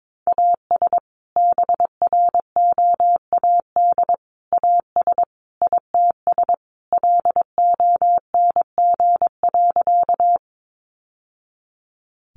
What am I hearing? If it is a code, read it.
Morse code, heard as a keyed tone, about 22 wpm, 700 Hz. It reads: AS BROAD AS ITS LONG.
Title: Sentences from Top 1000 Words 22wpm